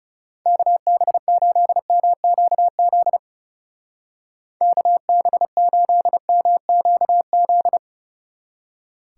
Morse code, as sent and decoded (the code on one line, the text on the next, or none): -.- -... ---.. -- --.- --.. / -.- -... ---.. -- --.- --..
KB8MQZ KB8MQZ